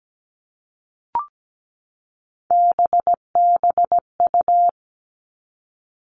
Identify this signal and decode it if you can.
Morse code, heard as a keyed tone, about 17 wpm, 700 Hz. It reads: BBU
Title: All Cut Numbers - 3-Digit 17wpm